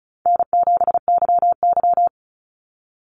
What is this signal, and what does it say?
Morse code, heard as a keyed tone, about 34 wpm, 700 Hz. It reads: N7YY
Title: NA SOTA Chaser Callsigns 35wpm